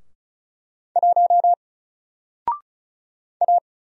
Morse code, heard as a keyed tone, about 35 wpm, 700 Hz. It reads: 1 A
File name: Single-Letter-Number-35wpm